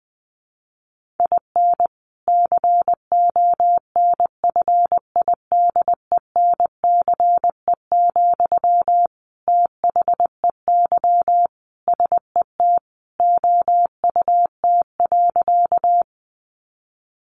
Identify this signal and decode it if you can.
Morse code, heard as a keyed tone, about 20 wpm, 700 Hz. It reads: IN CONFIDENCE, THEY SET OUT.
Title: Sentences from Top 1500 Words 20wpm